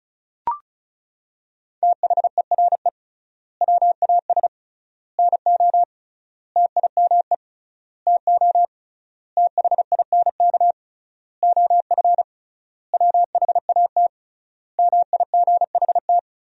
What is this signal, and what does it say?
Morse code, heard as a keyed tone, about 36 wpm, 700 Hz. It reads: THERE WAS NO TIME TO THINK OF WHAT MIGHT
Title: 3x Word-Spacing - Sentences from Top 400 Words - 35wpm